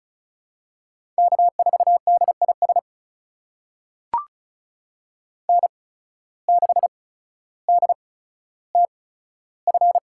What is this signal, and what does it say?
Morse code, heard as a keyed tone, about 36 wpm, spaced as for 15 wpm, 700 Hz. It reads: K4DIS N6DTF
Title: USA Call Signs - 5 Characters - 8x Character Spacing - 35wpm